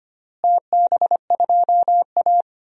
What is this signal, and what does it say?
Morse code, heard as a keyed tone, about 25 wpm, 700 Hz. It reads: TB2A
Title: DXCC-100 Call Signs - 4 Characters - Repeated 3x - 25wpm